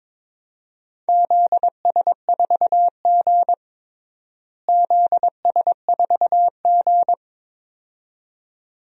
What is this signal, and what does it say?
Morse code, heard as a keyed tone, about 21 wpm, 700 Hz. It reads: ZS4G ZS4G
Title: DXCC-100 Call Signs - 4 Characters - Repeated 2x - 22wpm